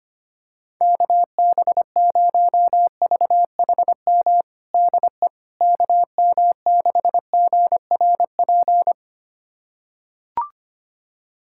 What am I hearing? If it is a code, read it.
Morse code, heard as a keyed tone, about 25 wpm, 700 Hz. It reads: KB0VHM DE KM6GRP
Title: USA Callsign DE Callsign - 6 Characters - 25wpm